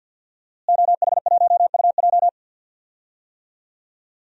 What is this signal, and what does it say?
Morse code, heard as a keyed tone, about 48 wpm, 700 Hz. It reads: OH1UJ
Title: DXCC-100 Call Signs - 5 Characters - Repeated 3x - 50wpm